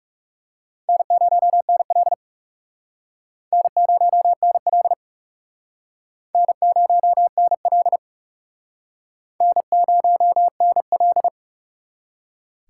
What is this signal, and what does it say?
Morse code, heard as a keyed tone, about 38 wpm, 700 Hz. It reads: N0NL N0NL N0NL N0NL
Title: Speed Racing - USA 1x2 Call Signs 30wpm